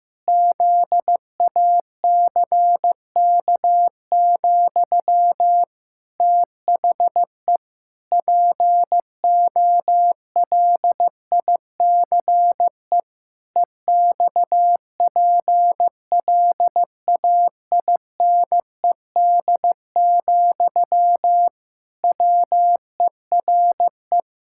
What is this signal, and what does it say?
Morse code, heard as a keyed tone, about 15 wpm, 700 Hz. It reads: ZACK, THE POLICE EXPLAINED, WERE
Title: Pangram Challenge v2 - 15wpm